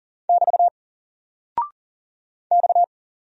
Morse code, heard as a keyed tone, about 40 wpm, 700 Hz. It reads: <BT> X
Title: Single Character (Letter:Number:Punctuation:Prosign) 40wpm